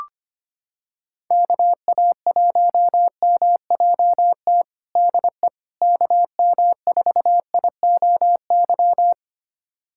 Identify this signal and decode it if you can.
Morse code, heard as a keyed tone, about 25 wpm, 700 Hz. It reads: KA1MJT DE KM4IOY
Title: USA Callsign DE Callsign - 6 Characters - 25wpm